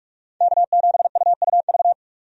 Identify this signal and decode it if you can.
Morse code, heard as a keyed tone, about 42 wpm, 700 Hz. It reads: K7UUV
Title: USA Call Signs - 5 Characters - 45wpm